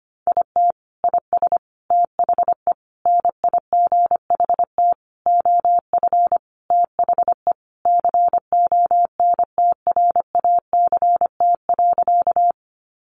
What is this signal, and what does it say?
Morse code, heard as a keyed tone, about 25 wpm, 700 Hz. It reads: IT IS THE NIGHT OF THE CONTRACT.
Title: Sentences from Top 1400 Words 25wpm